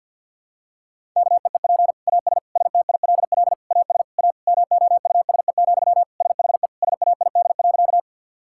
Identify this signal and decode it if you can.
Morse code, heard as a keyed tone, about 50 wpm, 700 Hz. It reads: KEEP AS STILL AS A MOUSE, SHE SAID.
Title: Sentences from Top 1000 Words - Review Part 5 - 50wpm